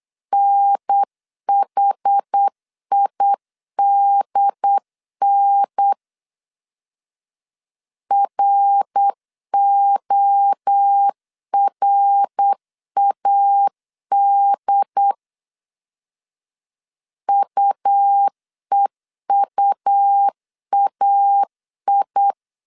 Morse code, heard as a keyed tone, about 8 wpm, 800 Hz. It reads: NHIDN RORAD UEUAI
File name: CW / DET / 1 3 1500C S B 8wpm 4141